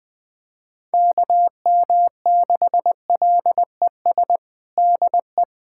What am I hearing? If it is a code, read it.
Morse code, heard as a keyed tone, about 20 wpm, 700 Hz. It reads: KM6LES DE
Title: USA Callsign DE Callsign - 6 Characters - 20wpm